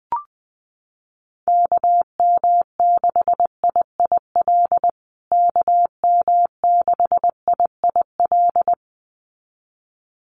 Morse code, heard as a keyed tone, about 20 wpm, 700 Hz. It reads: KM6IIL KM6IIL
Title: USA Call Signs Repeated 2x 20wpm